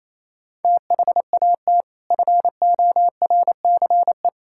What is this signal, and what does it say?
Morse code, heard as a keyed tone, about 28 wpm, 700 Hz. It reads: THAT FORCE
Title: Sentences from Top 1000 Words - Review Part 5 - 28wpm